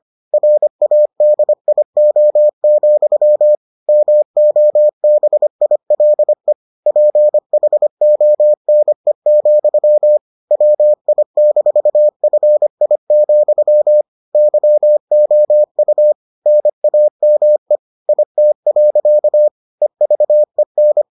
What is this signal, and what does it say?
Morse code, heard as a keyed tone, about 25 wpm, 600 Hz. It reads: RADIO, MOBILE PHONE, WI-FI, YOU NAME IT. EVEN